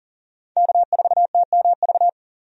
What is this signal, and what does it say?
Morse code, heard as a keyed tone, about 40 wpm, 700 Hz. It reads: K4TMV
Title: USA Call Signs - 5 Characters - 40wpm